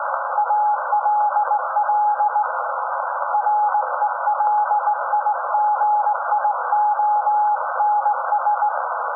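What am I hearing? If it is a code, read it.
Morse code, heard as a keyed tone, about 19 wpm, 840 Hz. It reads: ALL RELIGIONS
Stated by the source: SAQ Grimeton Radio, Alexanderson 200 kW alternator on 17.2 kHz, received in Potsdam with a very strong signal on World Radio Day.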